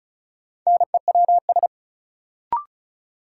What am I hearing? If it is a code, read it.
Morse code, heard as a keyed tone, about 36 wpm, 700 Hz. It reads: NEWS
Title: Sets of 5 Words - New Word Drill - 35wpm